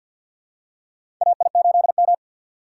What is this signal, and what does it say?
Morse code, heard as a keyed tone, about 52 wpm, 700 Hz. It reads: AI8M
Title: USA Call Signs - 4 Characters - 2x Character Spacing - 50wpm